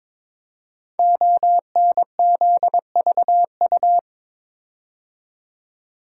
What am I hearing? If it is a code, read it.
Morse code, heard as a keyed tone, about 22 wpm, 700 Hz. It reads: ONZVU
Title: Code Groups - 5 Letter:Numbers - 22wpm